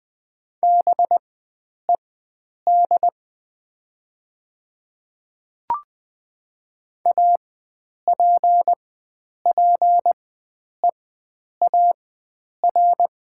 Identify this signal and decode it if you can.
Morse code, heard as a keyed tone, about 20 wpm, spaced as for 9 wpm, 700 Hz. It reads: BED APPEAR
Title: 4x Character Spacing - Top 500 Words - 20wpm